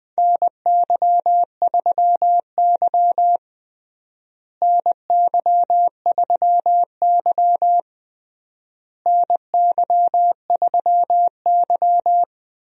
Morse code, heard as a keyed tone, about 20 wpm, 700 Hz. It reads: NY3Y NY3Y NY3Y
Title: USA Call Signs - 4 Characters - Repeated 3x - 20wpm